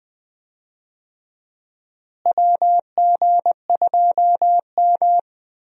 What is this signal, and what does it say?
Morse code, heard as a keyed tone, about 20 wpm, 700 Hz. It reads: WG2M
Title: USA Call Signs - 4 Characters - Repeated 2x - 20wpm